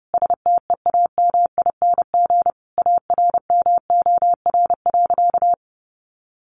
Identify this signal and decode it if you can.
Morse code, heard as a keyed tone, about 30 wpm, 700 Hz. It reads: STEAMING ARMOR.
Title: A Sound Of Thunder 30wpm